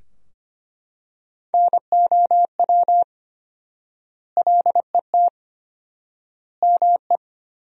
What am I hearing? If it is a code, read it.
Morse code, heard as a keyed tone, about 25 wpm, 700 Hz. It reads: NOW LET ME
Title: 4x Word-Spacing - Sentences from Top 300 Words - 25wpm